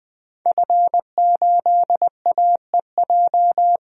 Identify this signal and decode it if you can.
Morse code, heard as a keyed tone, about 20 wpm, 700 Hz. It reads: F8AEJ
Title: DXCC-100 Call Signs - 5 Characters - 20wpm